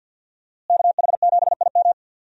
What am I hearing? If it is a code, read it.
Morse code, heard as a keyed tone, about 51 wpm, 700 Hz. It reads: KH7IM
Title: USA Call Signs - 5 Characters - Repeated 3x - 50wpm